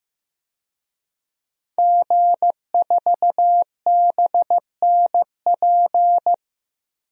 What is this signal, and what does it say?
Morse code, heard as a keyed tone, about 15 wpm, 700 Hz. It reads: G4BNP
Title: DXCC-100 Call Signs - 5 Characters - Repeated 3x - 15wpm